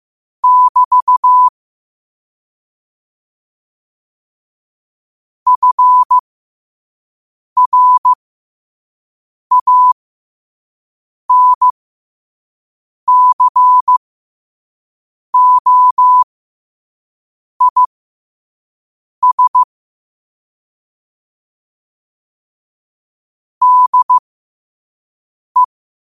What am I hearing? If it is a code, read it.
Morse code, heard as a keyed tone, about 15 wpm, spaced as for 5 wpm, 1000 Hz. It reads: <BT> FRANCOIS DE
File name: qotdfiles / 5wpm